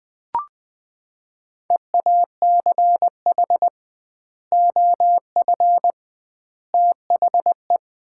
Morse code, heard as a keyed tone, about 21 wpm, 700 Hz. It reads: EACH OF THE
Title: 2x Word-Spacing - Sets of 3 Words - 20wpm